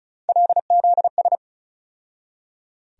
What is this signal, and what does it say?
Morse code, heard as a keyed tone, about 36 wpm, 700 Hz. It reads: LZS
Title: USA Call Sign Postfixes - 3 Letters 35wpm